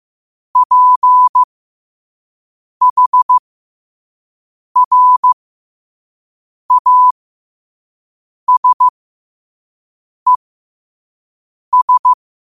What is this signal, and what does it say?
Morse code, heard as a keyed tone, about 15 wpm, spaced as for 5 wpm, 1000 Hz. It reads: PHRASES
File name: qotdfiles / 5wpm